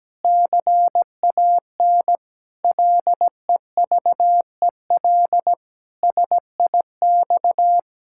Morse code, heard as a keyed tone, about 17 wpm, 700 Hz. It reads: CAN LEVEL SIX
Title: Pangram Challenge v2 - 17wpm